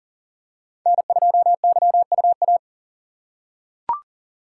Morse code, heard as a keyed tone, about 40 wpm, 690 Hz. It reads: N2YUA
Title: USA Call Signs - 5 Characters - 3x Character Spacing - 40wpm